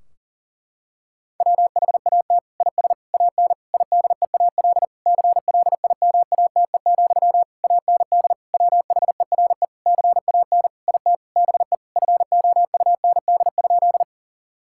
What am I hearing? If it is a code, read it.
Morse code, heard as a keyed tone, about 40 wpm, 700 Hz. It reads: WHAT IS AN IDEAL CLIMATE, AND WHERE CAN IT BE FOUND?
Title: Sentences from Top 1600 Words 40wpm